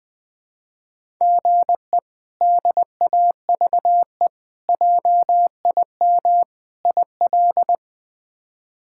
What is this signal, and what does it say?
Morse code, heard as a keyed tone, about 20 wpm, 700 Hz. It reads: GE DAVE JIM IL
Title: K1USN SST QSO Practice - 20wpm